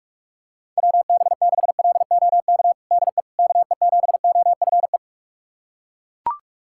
Morse code, heard as a keyed tone, about 41 wpm, 690 Hz. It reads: WB6LOK DE KE7OFE
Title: USA Callsign DE Callsign - 6 Characters - 45wpm